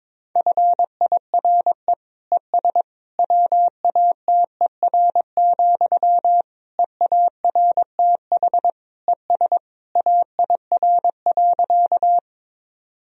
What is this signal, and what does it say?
Morse code, heard as a keyed tone, about 22 wpm, 700 Hz. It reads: FIRE ES WATER, EARTH ES AIR.
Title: Sentences from Top 400 Words and Abbreviations 22wpm